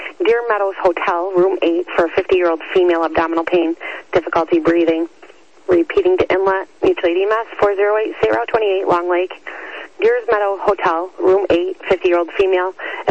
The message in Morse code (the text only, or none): none